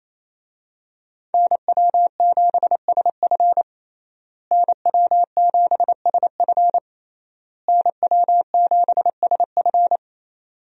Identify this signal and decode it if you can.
Morse code, heard as a keyed tone, about 28 wpm, 700 Hz. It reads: NW7SF NW7SF NW7SF
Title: USA Call Signs - 5 Characters - Repeated 3x - 28wpm